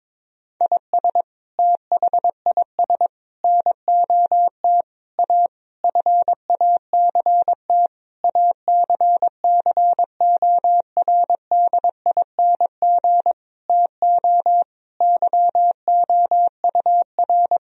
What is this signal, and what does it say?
Morse code, heard as a keyed tone, about 22 wpm, 700 Hz. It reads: IS THIS NOT A FACT ACCORDING TO YOUR
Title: Sentences from Top 1000 Words - Review Part 4 - 22wpm